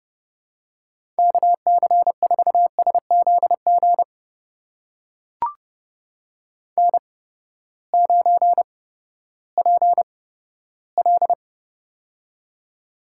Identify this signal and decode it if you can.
Morse code, heard as a keyed tone, about 30 wpm, spaced as for 15 wpm, 700 Hz. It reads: KC4SZG N9PL
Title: USA Call Signs - 8x Character Spacing - 30wpm